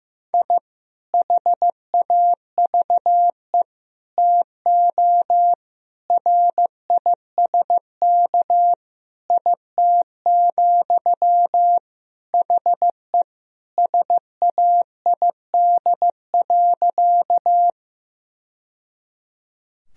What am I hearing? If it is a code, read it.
Morse code, heard as a keyed tone, about 16 wpm, 700 Hz. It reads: I HAVE TO RISK IT, HE SAID.
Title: Sentences from Top 1000 Words - Review Part 4 - 15wpm